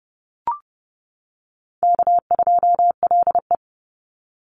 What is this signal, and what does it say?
Morse code, heard as a keyed tone, about 30 wpm, 700 Hz. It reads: K2LE
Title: Easy Call Signs - 30wpm